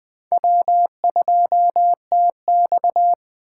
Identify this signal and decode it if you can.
Morse code, heard as a keyed tone, about 20 wpm, 700 Hz. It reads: W2TX
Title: USA Call Signs - 4 Characters - 3x Character Spacing - 20wpm